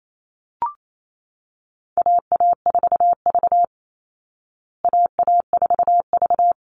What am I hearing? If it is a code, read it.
Morse code, heard as a keyed tone, about 28 wpm, 700 Hz. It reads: AA4V AA4V
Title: Call Signs Repeated 2x - 28wpm